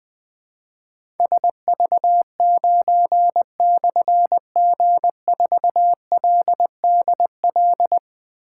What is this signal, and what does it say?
Morse code, heard as a keyed tone, about 20 wpm, 700 Hz. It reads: SV9/G4LDL
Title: Difficult Call Signs 20wpm